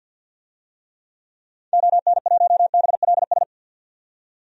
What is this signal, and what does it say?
Morse code, heard as a keyed tone, about 50 wpm, 700 Hz. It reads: ON1BLS
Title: DXCC-100 Call Signs - 6 Characters - 50wpm